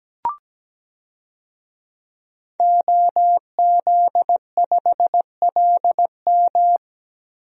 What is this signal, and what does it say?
Morse code, heard as a keyed tone, about 17 wpm, 700 Hz. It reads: OZ5LM
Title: DXCC-100 Call Signs - 5 Characters - Repeated 3x - 17wpm